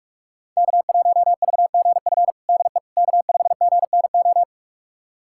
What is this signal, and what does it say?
Morse code, heard as a keyed tone, about 44 wpm, 700 Hz. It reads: K1VGF DE K5GNO
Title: USA Callsign DE Callsign - 5 Characters - 45wpm